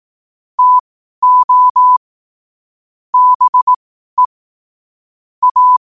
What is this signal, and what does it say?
Morse code, heard as a keyed tone, about 18 wpm, spaced as for 12 wpm, 1000 Hz. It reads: TO BE A
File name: qotdfiles / 12wpm